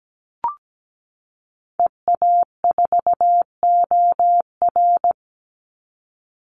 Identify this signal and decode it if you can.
Morse code, heard as a keyed tone, about 17 wpm, 700 Hz. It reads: EA4OR
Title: Call Signs - 17wpm